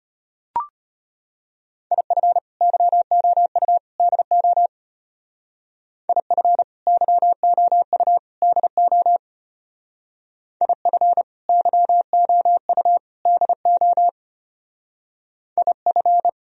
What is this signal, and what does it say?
Morse code, heard as a keyed tone, about 31 wpm, 700 Hz. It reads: IF YOU DO IF YOU DO IF YOU DO IF
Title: Speed-Racing - Sets of 3 Words 25wpm